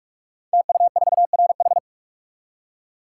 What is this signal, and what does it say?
Morse code, heard as a keyed tone, about 44 wpm, 700 Hz. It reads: TU4RH
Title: Code Groups - 5 Letter:Numbers - 45wpm